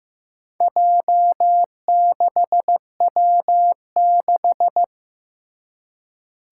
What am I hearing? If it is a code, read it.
Morse code, heard as a keyed tone, about 15 wpm, 700 Hz. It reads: J6W6
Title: Four Challenging Letter-Number Combinations 15wpm